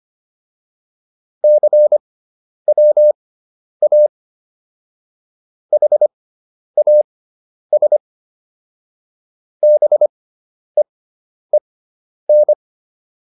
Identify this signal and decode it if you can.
Morse code, heard as a keyed tone, about 25 wpm, spaced as for 10 wpm, 600 Hz. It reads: CWA HAS BEEN